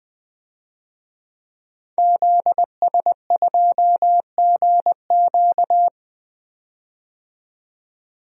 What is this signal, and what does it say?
Morse code, heard as a keyed tone, about 20 wpm, 700 Hz. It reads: ZS2GQ
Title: DXCC-100 Call Signs - 5 Characters - 20wpm